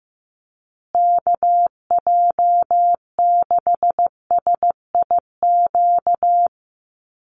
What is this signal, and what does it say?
Morse code, heard as a keyed tone, about 15 wpm, 700 Hz. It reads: KJ6SIQ
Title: USA Call Signs Repeated 2x 15wpm